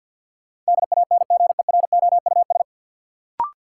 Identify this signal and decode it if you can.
Morse code, heard as a keyed tone, about 51 wpm, 700 Hz. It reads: DANGEROUS
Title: Sentences from Top 1200 Words - New Word Drill - 50wpm